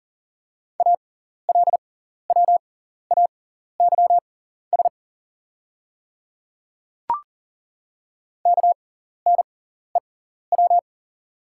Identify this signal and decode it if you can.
Morse code, heard as a keyed tone, about 40 wpm, spaced as for 14 wpm, 690 Hz. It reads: ALWAYS KNEW
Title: 6x Character Spacing - Top 500 Words - 40wpm